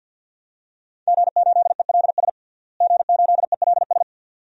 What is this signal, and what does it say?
Morse code, heard as a keyed tone, about 52 wpm, 700 Hz. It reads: G8ELS G8ELS
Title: DXCC-100 Call Signs - 5 Characters - Repeated 2x - 50wpm